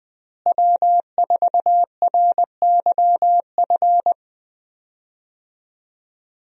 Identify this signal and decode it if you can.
Morse code, heard as a keyed tone, about 20 wpm, 700 Hz. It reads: W4RYF
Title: USA Call Signs - 5 Characters - Repeated 2x - 20wpm